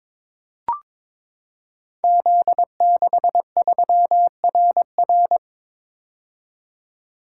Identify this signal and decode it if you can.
Morse code, heard as a keyed tone, about 22 wpm, 700 Hz. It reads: Z63RR
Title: Code Groups - 5 Letter:Numbers - 22wpm